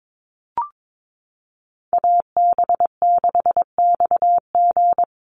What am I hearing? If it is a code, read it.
Morse code, heard as a keyed tone, about 22 wpm, 700 Hz. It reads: AB6XG
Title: Call Signs - 22wpm